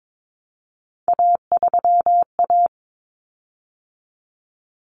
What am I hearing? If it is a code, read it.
Morse code, heard as a keyed tone, about 22 wpm, 700 Hz. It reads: A3A
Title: Common Cut Numbers - 3-Digit 22wpm